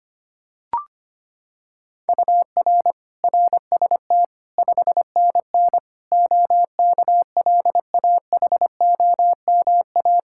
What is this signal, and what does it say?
Morse code, heard as a keyed tone, about 25 wpm, 700 Hz. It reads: UR RST 5NN OKLAHOMA
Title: Practice QSO - Exchange 1 - Beginning - 25wpm